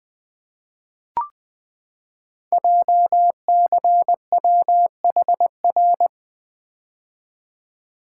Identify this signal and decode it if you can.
Morse code, heard as a keyed tone, about 20 wpm, 700 Hz. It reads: JCWHR
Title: Code Groups - 5 Letters - 20wpm